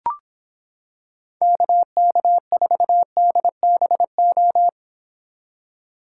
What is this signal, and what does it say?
Morse code, heard as a keyed tone, about 26 wpm, 700 Hz. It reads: KK4DBO